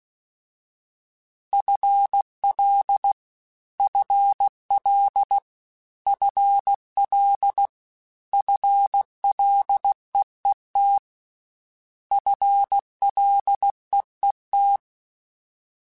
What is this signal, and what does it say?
Morse code, heard as a keyed tone, about 16 wpm, 800 Hz. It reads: FL FL FL FLEET FLEET
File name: FL / FL mp3s / FL Prefixes 3x2 Twice @15